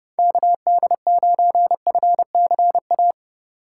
Morse code, heard as a keyed tone, about 29 wpm, 700 Hz. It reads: KD9FCA